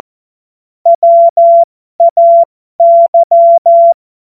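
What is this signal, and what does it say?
Morse code, heard as a keyed tone, about 14 wpm, spaced as for 12 wpm, 680 Hz. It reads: WAY
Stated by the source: Morse joke practice at effective 60, Farnsworth 70.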